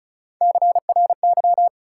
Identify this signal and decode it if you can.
Morse code, heard as a keyed tone, about 36 wpm, 700 Hz. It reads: CRY
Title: Sets of 5 Words Encore - New Word Drill - 35wpm